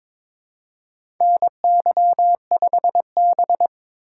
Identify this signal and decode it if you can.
Morse code, heard as a keyed tone, about 23 wpm, 700 Hz. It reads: NY5B
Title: USA Call Signs - 3x Character Spacing - 22wpm